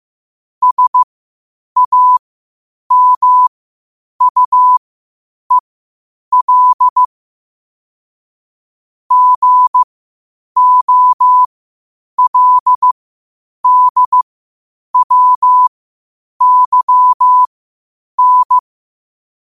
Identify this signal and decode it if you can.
Morse code, heard as a keyed tone, about 15 wpm, spaced as for 8 wpm, 1000 Hz. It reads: SAMUEL GOLDWYN
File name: qotdfiles / 8wpm